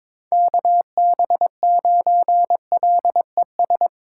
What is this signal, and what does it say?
Morse code, heard as a keyed tone, about 22 wpm, 700 Hz. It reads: KB9LES
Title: USA Call Signs - 6 Characters - 2x Character Spacing - 22wpm